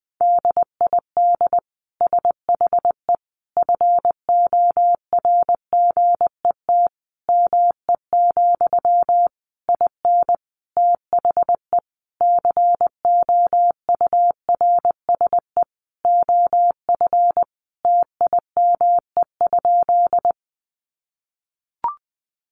Morse code, heard as a keyed tone, about 20 wpm, 700 Hz. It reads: DID SHE FORGET ME, IN THE COURSE OF TIME?
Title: Sentences from Top 1200 Words 20wpm